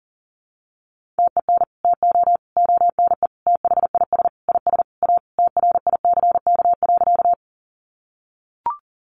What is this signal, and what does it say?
Morse code, heard as a keyed tone, about 40 wpm, 700 Hz. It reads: TEN TO ONE THIS IS A TRICK.
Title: Sentences from Top 1500 Words 40wpm